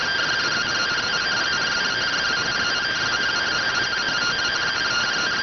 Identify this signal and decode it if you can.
Two-tone teleprinter signal, about 170 Hz shift.